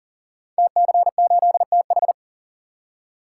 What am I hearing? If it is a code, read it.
Morse code, heard as a keyed tone, about 38 wpm, 700 Hz. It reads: TC8TH
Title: Code Groups - 5 Letter:Numbers - 40wpm